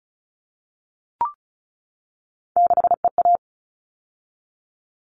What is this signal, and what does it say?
Morse code, heard as a keyed tone, about 34 wpm, 700 Hz. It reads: 6EA
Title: All Cut Numbers - 3-Digit 35wpm